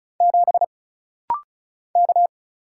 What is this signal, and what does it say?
Morse code, heard as a keyed tone, about 35 wpm, 700 Hz. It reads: 7 K
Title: Single Letter-Number - Rapid-Fire 35wpm